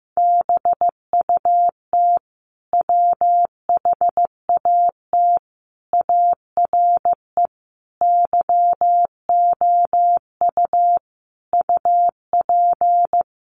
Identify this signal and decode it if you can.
Morse code, heard as a keyed tone, about 14 wpm, 700 Hz. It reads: BUT WHAT ARE YOU UP
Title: Sentences from Top 100 Words 15wpm